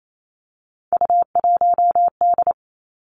Morse code, heard as a keyed tone, about 28 wpm, 700 Hz. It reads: U1D
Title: Three Challenging Letter-Number Combinations 28wpm